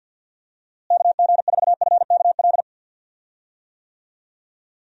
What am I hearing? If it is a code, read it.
Morse code, heard as a keyed tone, about 47 wpm, 700 Hz. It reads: KG4FKL
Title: USA Call Signs - 6 Characters - Repeated 2x - 50wpm